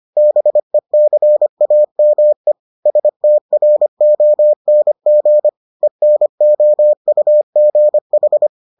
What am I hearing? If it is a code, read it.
Morse code, heard as a keyed tone, about 25 wpm, 600 Hz. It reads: BECAME STRONG ENOUGH